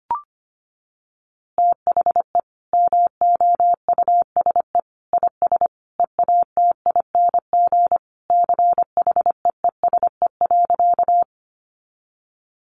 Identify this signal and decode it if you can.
Morse code, heard as a keyed tone, about 25 wpm, 700 Hz. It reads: THE MOUSE IS EATING CHEESE.
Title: Easy Phrases 25wpm